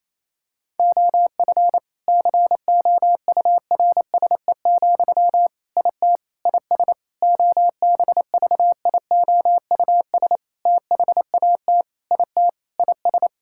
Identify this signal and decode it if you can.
Morse code, heard as a keyed tone, about 28 wpm, 700 Hz. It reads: OF COURSE, IT IS OBVIOUS THAT IT IS
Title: Sentences from Top 1600 Words 28wpm